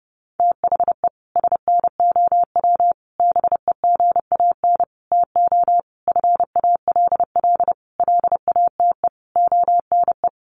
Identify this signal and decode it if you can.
Morse code, heard as a keyed tone, about 30 wpm, 700 Hz. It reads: THE SNOW BEGAN TO FALL LATE ONE
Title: Sentences from Top 1500 Words 30wpm